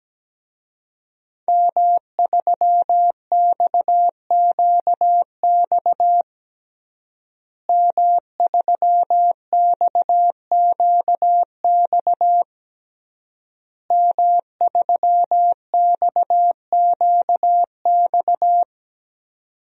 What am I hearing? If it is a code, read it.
Morse code, heard as a keyed tone, about 17 wpm, 700 Hz. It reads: M3XQX M3XQX M3XQX
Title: DXCC-100 Call Signs - 5 Characters - Repeated 3x - 17wpm